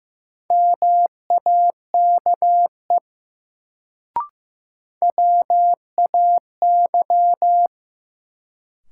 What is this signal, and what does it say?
Morse code, heard as a keyed tone, about 15 wpm, 700 Hz. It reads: MAKE WAY
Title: Sentences from Top 100 Words - New Word Drill - 15wpm